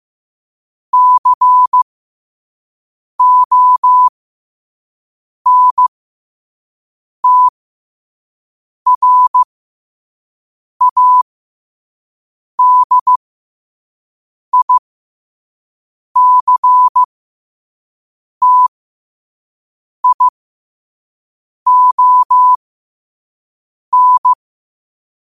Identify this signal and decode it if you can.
Morse code, heard as a keyed tone, about 15 wpm, spaced as for 5 wpm, 1000 Hz. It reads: CONTRADICTION